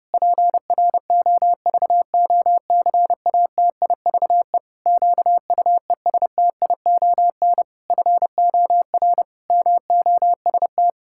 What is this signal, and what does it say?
Morse code, heard as a keyed tone, about 32 wpm, 700 Hz. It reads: PROVOCATIVE QUESTION FOR MOST
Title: Pangram Challenge v2 - 30wpm